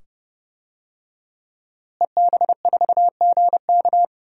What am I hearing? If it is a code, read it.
Morse code, heard as a keyed tone, about 30 wpm, 700 Hz. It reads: EB4GK
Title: DXCC-100 Call Signs - 5 Characters - Repeated 2x - 30wpm